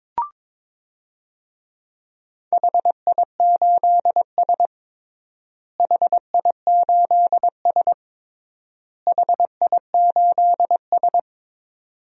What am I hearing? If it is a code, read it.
Morse code, heard as a keyed tone, about 22 wpm, 700 Hz. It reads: HI8S HI8S HI8S
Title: DXCC-100 Call Signs - 4 Characters - Repeated 3x - 22wpm